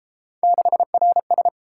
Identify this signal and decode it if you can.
Morse code, heard as a keyed tone, about 31 wpm, 700 Hz. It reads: 6RS